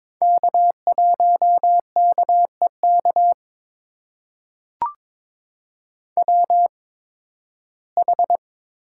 Morse code, heard as a keyed tone, about 21 wpm, spaced as for 14 wpm, 700 Hz. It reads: K1KEK WH
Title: USA Call Signs - 5 Characters - 8x Character Spacing - 22wpm